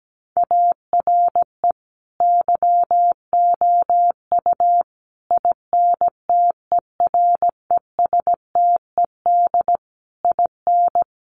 Morse code, heard as a keyed tone, about 17 wpm, 700 Hz. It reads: ARE YOU INTERESTED IN
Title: Sentences from Top 1200 Words 17wpm